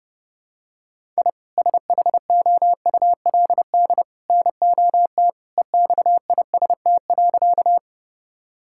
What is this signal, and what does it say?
Morse code, heard as a keyed tone, about 30 wpm, 700 Hz. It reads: I SHOULD NOT EXIST.
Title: Sentences from Top 1000 Words - Review Part 5 - 30wpm